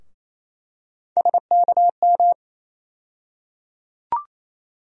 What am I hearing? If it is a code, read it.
Morse code, heard as a keyed tone, about 28 wpm, 700 Hz. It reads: SKM
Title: USA Call Sign Postfixes - 3 Letters 28wpm